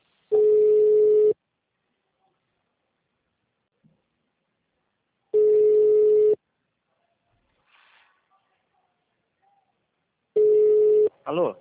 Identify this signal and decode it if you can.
Ringback tone (Europe, 425 Hz), with speech.